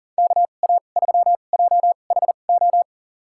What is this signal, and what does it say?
Morse code, heard as a keyed tone, about 41 wpm, spaced as for 29 wpm, 690 Hz. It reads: KA3JHO
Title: USA Call Signs - 2x Character Spacing - 40wpm